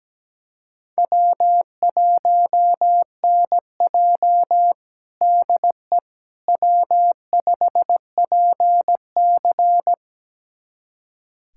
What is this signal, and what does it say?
Morse code, heard as a keyed tone, about 17 wpm, 700 Hz. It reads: W1NJ DE W5PC
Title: USA Callsign DE Callsign - 4 Characters - 17wpm